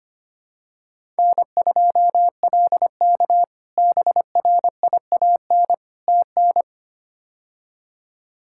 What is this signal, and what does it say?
Morse code, heard as a keyed tone, about 25 wpm, 700 Hz. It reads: N2LK BRIAN TN
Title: K1USN SST QSO Practice - 25wpm